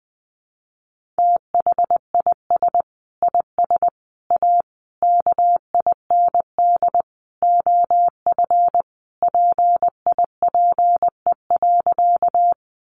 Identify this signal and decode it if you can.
Morse code, heard as a keyed tone, about 19 wpm, 700 Hz. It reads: THIS IS A KIND OF PIPE.
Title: Sentences from Top 1500 Words 20wpm